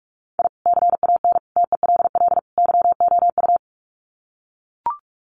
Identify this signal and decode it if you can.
Morse code, heard as a keyed tone, about 44 wpm, 700 Hz. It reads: I CAN TELL YOU
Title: Sets of 4 Words 45wpm